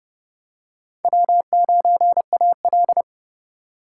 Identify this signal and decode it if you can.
Morse code, heard as a keyed tone, about 32 wpm, 700 Hz. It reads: W9AL
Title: USA Call Signs - 3x Character Spacing - 30wpm